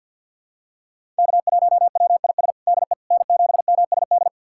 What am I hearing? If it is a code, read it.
Morse code, heard as a keyed tone, about 50 wpm, 700 Hz. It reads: K1WIS DE N7MSD
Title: USA Callsign DE Callsign 50wpm